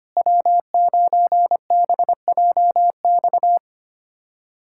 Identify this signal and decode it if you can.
Morse code, heard as a keyed tone, about 25 wpm, 700 Hz. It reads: W9BJX
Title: USA Callsign DE Callsign - 5 Characters - 25wpm